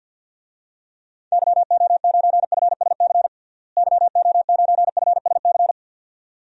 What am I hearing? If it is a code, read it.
Morse code, heard as a keyed tone, about 51 wpm, 700 Hz. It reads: YO9FSC YO9FSC
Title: DXCC-100 Call Signs - 6 Characters - Repeated 2x - 50wpm